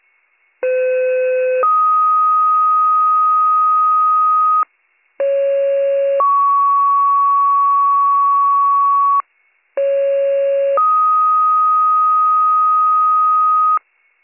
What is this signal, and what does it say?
Two-tone page, heard as a one-second tone, then a three-second tone, about 510 then 1250 Hz, about 550 then 1120 Hz, about 550 then 1250 Hz. It is heard from 0.6 to 13.8 s.